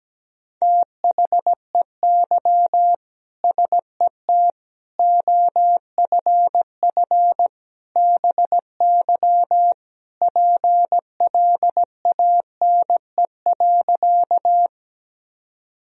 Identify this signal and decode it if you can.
Morse code, heard as a keyed tone, about 17 wpm, 700 Hz. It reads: THEY SET OFF BY PLANE.
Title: Sentences from Top 1000 Words - Review Part 3 - 17wpm